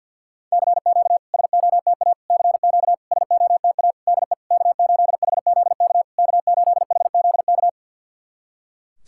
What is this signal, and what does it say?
Morse code, heard as a keyed tone, about 51 wpm, 690 Hz. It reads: CQ SOTA CQ SOTA DE K8HZK K8HZK
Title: SOTA QSO Practice 50wpm